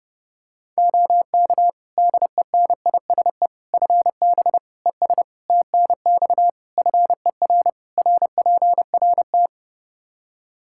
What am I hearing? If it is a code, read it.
Morse code, heard as a keyed tone, about 32 wpm, 700 Hz. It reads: OK DENISE FB ES TNX FER RPRT
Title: Practice QSO - Exchange 2 - Info - 30wpm